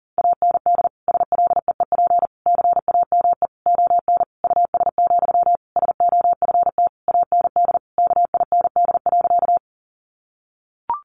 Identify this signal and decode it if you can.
Morse code, heard as a keyed tone, about 40 wpm, 700 Hz. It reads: AND SLEEP CAME ON US, SOFT AND KIND.
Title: Sentences from Top 1500 Words 40wpm